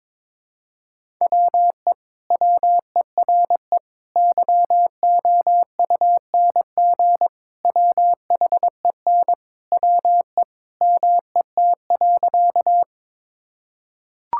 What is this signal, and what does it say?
Morse code, heard as a keyed tone, about 22 wpm, 700 Hz. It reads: WE WERE YOUNG WHEN WE MET.
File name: practice / 22wpm / Sentences with 2-Word Phrases - 22wpm